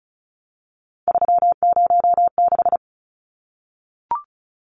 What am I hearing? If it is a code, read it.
Morse code, heard as a keyed tone, about 34 wpm, 700 Hz. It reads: 306
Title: Numbers - 3-Digit 35wpm